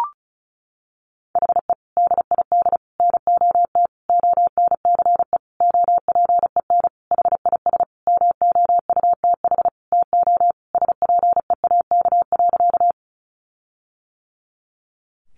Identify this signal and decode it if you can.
Morse code, heard as a keyed tone, about 34 wpm, 700 Hz. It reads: HE DID NOT ONCE OPEN HIS MOUTH TO SPEAK.
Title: Sentences from Top 1500 Words 35wpm